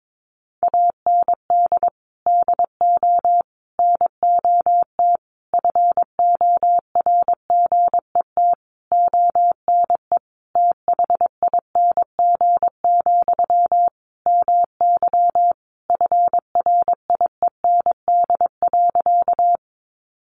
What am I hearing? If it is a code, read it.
Morse code, heard as a keyed tone, about 22 wpm, 700 Hz. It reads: AND DO NOT FORGET ONE THING, MY FRIEND.
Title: Sentences from Top 1200 Words 22wpm